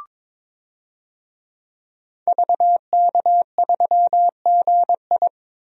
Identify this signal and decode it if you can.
Morse code, heard as a keyed tone, about 22 wpm, 700 Hz. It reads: VK3GI
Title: DXCC-100 Call Signs - 5 Characters - 22wpm